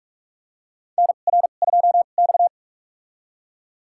Morse code, heard as a keyed tone, about 41 wpm, spaced as for 31 wpm, 700 Hz. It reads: NR2X
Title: USA Call Signs - 4 Characters - 2x Character Spacing - 45wpm